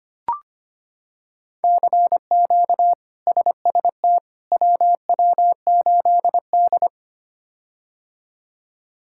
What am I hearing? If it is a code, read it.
Morse code, heard as a keyed tone, about 25 wpm, 700 Hz. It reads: CQ SST WW8D
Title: K1USN SST QSO Practice - 25wpm